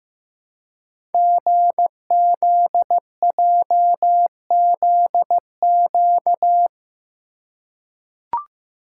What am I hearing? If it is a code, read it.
Morse code, heard as a keyed tone, about 15 wpm, 700 Hz. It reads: GZJZQ
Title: Code Groups - 5 Letter:Numbers - 15wpm